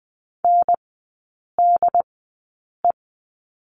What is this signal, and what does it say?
Morse code, heard as a keyed tone, about 20 wpm, spaced as for 8 wpm, 700 Hz. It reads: NDE
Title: 067 Lesson 12e - Letter d - Three Characters 20wpm